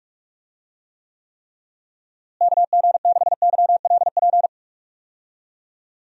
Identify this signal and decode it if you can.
Morse code, heard as a keyed tone, about 46 wpm, 700 Hz. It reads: KG6YLP
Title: USA Call Signs - 6 Characters - Repeated 3x - 45wpm